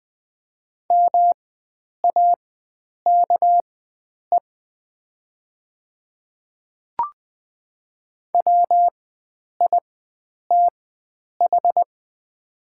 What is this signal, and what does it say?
Morse code, heard as a keyed tone, about 20 wpm, spaced as for 9 wpm, 700 Hz. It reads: MAKE WITH
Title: 4x Character Spacing - Top 100 Words - 20wpm